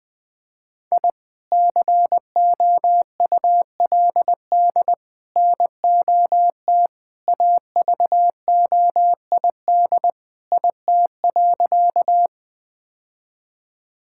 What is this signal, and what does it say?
Morse code, heard as a keyed tone, about 20 wpm, 700 Hz. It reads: I COULD NOT AVOID IT.
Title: Sentences from Top 1000 Words - Review Part 6 - 20wpm